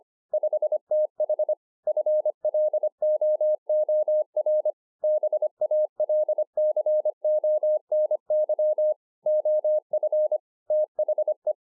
Morse code, heard as a keyed tone, about 24 wpm, 610 Hz. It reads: 5TH FLOOR BALCONY OF THE
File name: 2011-04-mp3 / BEGINNERS DIARY 25WPM